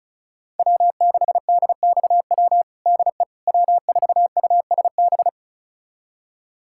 Morse code, heard as a keyed tone, about 36 wpm, 700 Hz. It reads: W6DXW DE W4USB
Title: USA Callsign DE Callsign - 5 Characters - 35wpm